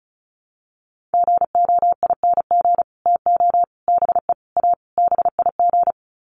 Morse code, heard as a keyed tone, about 34 wpm, 700 Hz. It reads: GOING TO BE A BIG
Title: Sets of 5 Words Encore2 35wpm